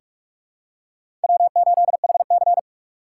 Morse code, heard as a keyed tone, about 46 wpm, 700 Hz. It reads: W8HC
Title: USA Call Signs - 2x Character Spacing - 45wpm